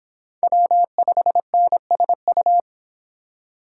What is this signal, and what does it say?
Morse code, heard as a keyed tone, about 30 wpm, 700 Hz. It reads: W5NSU